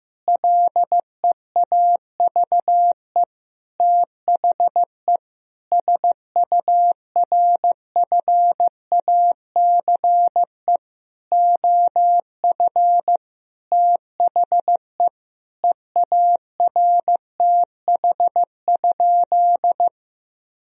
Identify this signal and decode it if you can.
Morse code, heard as a keyed tone, about 15 wpm, 700 Hz. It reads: LEAVE THE SURFACE OF THE EARTH?
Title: Sentences from Top 1000 Words - Review Part 3 - 15wpm